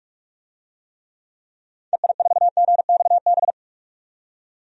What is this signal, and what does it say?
Morse code, heard as a keyed tone, about 46 wpm, 700 Hz. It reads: EI4GXB
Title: DXCC-100 Call Signs - 6 Characters - 45wpm